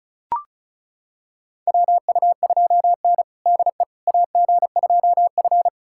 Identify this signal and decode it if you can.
Morse code, heard as a keyed tone, about 36 wpm, 700 Hz. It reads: WU2N DE AG2F
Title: USA Callsign DE Callsign - 4 Characters - 35wpm